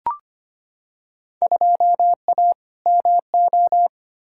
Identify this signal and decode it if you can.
Morse code, heard as a keyed tone, about 25 wpm, 700 Hz. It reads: 2A MO
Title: Field Day Exchanges Encore - 25wpm